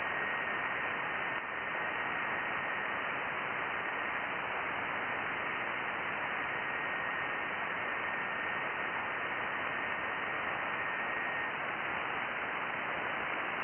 FT8.
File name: IC-7100